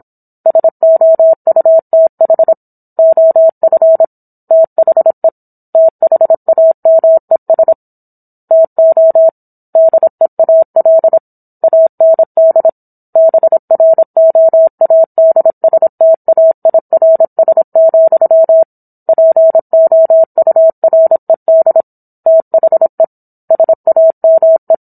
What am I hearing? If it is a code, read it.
Morse code, heard as a keyed tone, about 27 wpm, 650 Hz. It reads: SOUTH OF THE THAMES TO DEAL AND BROADSTAIRS, POURED THE SAME